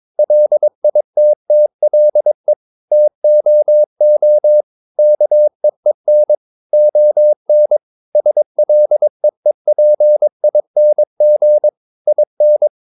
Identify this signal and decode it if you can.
Morse code, heard as a keyed tone, about 22 wpm, 600 Hz. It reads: LITTLE TOO KEEN ON SLEEPING IN
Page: Romanen Never af Ken Follet på engelsk.